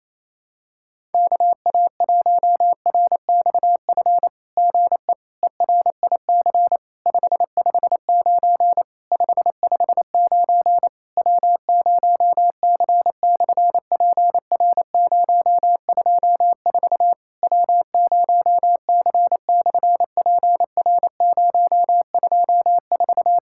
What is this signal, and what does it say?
Morse code, heard as a keyed tone, about 28 wpm, 700 Hz. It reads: KA1RXF GE ERIC 559 559 W0C/PR024 W0C/PR024
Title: SOTA QSO Practice 28wpm